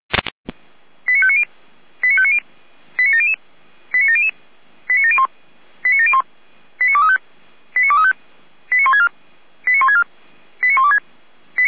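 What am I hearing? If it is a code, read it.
Five-tone selective call (ZVEI), heard as quick runs of five tones, tones of about 70 ms. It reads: 89590 89590 89700 89700 89701 89701 89235 89235 89164 89164 89126
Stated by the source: ZVEI five-tone call received by SDR.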